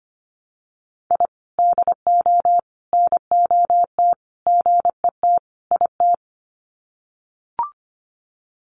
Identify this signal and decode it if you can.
Morse code, heard as a keyed tone, about 24 wpm, 700 Hz. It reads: I DO NOT GET IT
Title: Sets of 5 Words Encore 25wpm